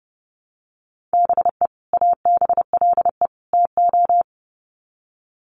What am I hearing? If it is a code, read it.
Morse code, heard as a keyed tone, about 30 wpm, 700 Hz. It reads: BE ABLE TO
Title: Sets of 3 Words 30wpm